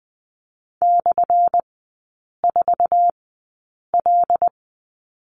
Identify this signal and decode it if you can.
Morse code, heard as a keyed tone, about 20 wpm, spaced as for 8 wpm, 700 Hz. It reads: /4L
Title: 178 Lesson 29e - Slash - Three Characters 20wpm